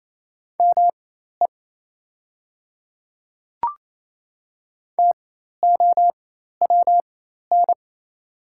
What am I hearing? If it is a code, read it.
Morse code, heard as a keyed tone, about 28 wpm, spaced as for 13 wpm, 690 Hz. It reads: ME TOWN
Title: 4x Character Spacing - Top 500 Words - 28wpm